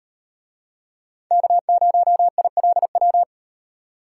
Morse code, heard as a keyed tone, about 34 wpm, 700 Hz. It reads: K0ILW